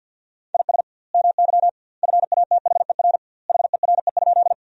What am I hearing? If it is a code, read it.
Morse code, heard as a keyed tone, about 50 wpm, 700 Hz. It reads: IS MY FATHER HERE?
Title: 2x Word-Spacing - Sentences from Top 300 Words - 50wpm